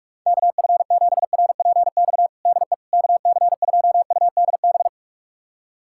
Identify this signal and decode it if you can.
Morse code, heard as a keyed tone, about 41 wpm, 700 Hz. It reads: KF7RPX DE KC2UDB
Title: USA Callsign DE Callsign - 6 Characters - 45wpm